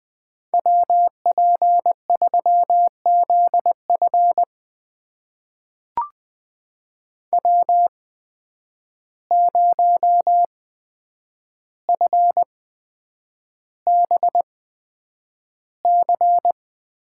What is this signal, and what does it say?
Morse code, heard as a keyed tone, about 20 wpm, spaced as for 9 wpm, 700 Hz. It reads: WP3ZF W0FBC
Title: USA Call Signs - 5 Characters - 8x Character Spacing - 20wpm